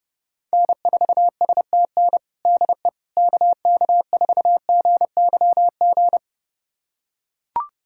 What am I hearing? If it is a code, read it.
Morse code, heard as a keyed tone, about 30 wpm, 700 Hz. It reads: N4STN DE KK4GYG
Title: USA Callsign DE Callsign 30wpm